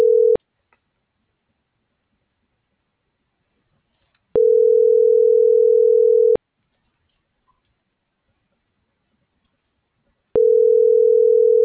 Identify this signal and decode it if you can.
Ringback tone (North America), 440 + 480 Hz.